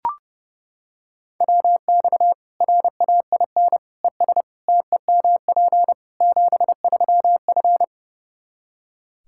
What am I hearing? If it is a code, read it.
Morse code, heard as a keyed tone, about 32 wpm, 700 Hz. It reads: WX RAIN ES TEMP 73F
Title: Practice QSO - Exchange 2 - Info - 30wpm